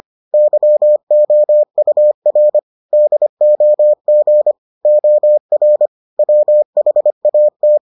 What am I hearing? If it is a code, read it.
Morse code, heard as a keyed tone, about 25 wpm, 600 Hz. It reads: YOUR DOG OR WHAT